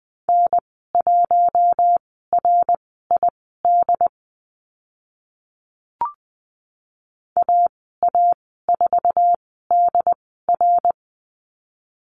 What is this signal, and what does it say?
Morse code, heard as a keyed tone, about 20 wpm, spaced as for 15 wpm, 700 Hz. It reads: N1RID AA4DR
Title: 076 Lesson 13g - Letter L - Call Signs 20wpm